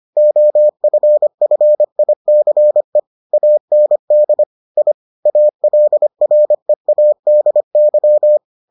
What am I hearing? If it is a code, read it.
Morse code, heard as a keyed tone, about 25 wpm, 600 Hz. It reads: OFFICE AND I ALREADY